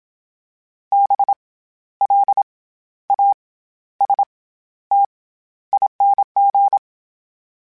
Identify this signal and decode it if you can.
Morse code, heard as a keyed tone, about 27 wpm, 800 Hz. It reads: B L A S T ING
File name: Special INGs-mp3s / Special INGs-mp3s / SlowToFaster ing @25